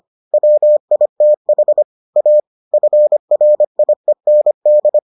Morse code, heard as a keyed tone, about 25 wpm, 600 Hz. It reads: WITH A FRIEND